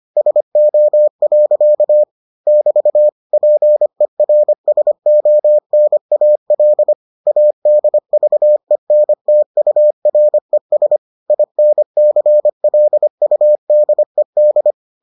Morse code, heard as a keyed tone, about 25 wpm, 600 Hz. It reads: SO. <BT> PERSONAL ADVENTURES INCLUDED